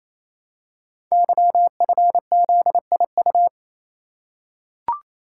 Morse code, heard as a keyed tone, about 27 wpm, 700 Hz. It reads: YFZIU
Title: Code Groups - 5 Letters - 28wpm